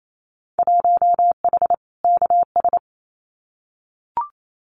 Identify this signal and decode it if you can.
Morse code, heard as a keyed tone, about 28 wpm, 700 Hz. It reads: 1H KS
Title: Winter Field Day Exchanges Encore 28wpm